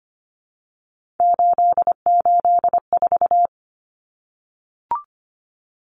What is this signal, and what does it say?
Morse code, heard as a keyed tone, about 25 wpm, 700 Hz. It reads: 884
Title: Common Cut Numbers - 3-Digit 25wpm